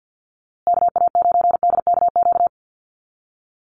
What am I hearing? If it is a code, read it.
Morse code, heard as a keyed tone, about 50 wpm, 700 Hz. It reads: KA9DKQ
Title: USA Call Signs Repeated 3x 50wpm